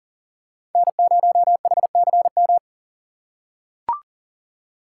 Morse code, heard as a keyed tone, about 40 wpm, 700 Hz. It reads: N0HCM
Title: USA Call Signs - 5 Characters - 6x Character Spacing - 40wpm